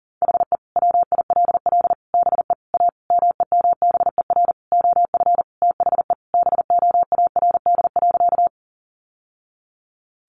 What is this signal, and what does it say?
Morse code, heard as a keyed tone, about 40 wpm, 700 Hz. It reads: HE WILL BE A MEMBER OF THE BOARD.
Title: Sentences from Top 1100 Words 40wpm